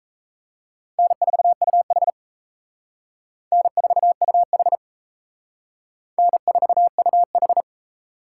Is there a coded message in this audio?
Yes: Morse code from 1.0 to 7.6 s, heard as a keyed tone, about 36 wpm, 700 Hz. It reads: N4UH N4UH N4UH